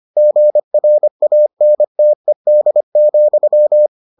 Morse code, heard as a keyed tone, about 25 wpm, 600 Hz. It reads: GRANTED,